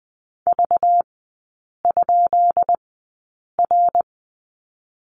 Morse code, heard as a keyed tone, about 20 wpm, spaced as for 8 wpm, 690 Hz. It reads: V?R
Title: 212 Lesson 34e - Number 8 - Three Characters 20wpm